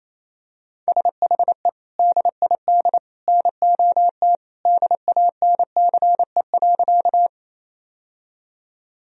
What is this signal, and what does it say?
Morse code, heard as a keyed tone, about 28 wpm, 700 Hz. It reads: SHE DID NOT DANCE.
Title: Sentences from Top 1000 Words - Review Part 4 - 28wpm